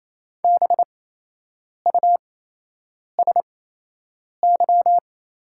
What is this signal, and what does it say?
Morse code, heard as a keyed tone, about 28 wpm, spaced as for 8 wpm, 700 Hz. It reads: BUSY
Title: 8x Character Spacing - Top 500 Words - 28wpm